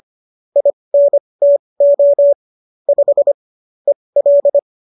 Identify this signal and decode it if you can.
Morse code, heard as a keyed tone, about 25 wpm, spaced as for 20 wpm, 570 Hz. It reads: INTO 5 EL